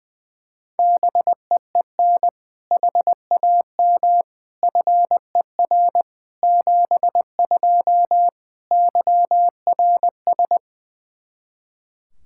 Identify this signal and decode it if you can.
Morse code, heard as a keyed tone, about 21 wpm, 700 Hz. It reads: BEEN HAM FER 72 YRS
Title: Practice QSO - Exchange 3 - More Info - 20wpm